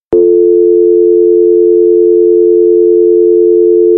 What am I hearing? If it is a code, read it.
Dial tone (North America), 350 + 440 Hz.
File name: america-dial-tone